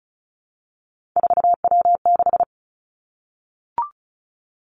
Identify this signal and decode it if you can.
Morse code, heard as a keyed tone, about 36 wpm, 700 Hz. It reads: 4W6
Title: Three Challenging Letter-Number Combinations 35wpm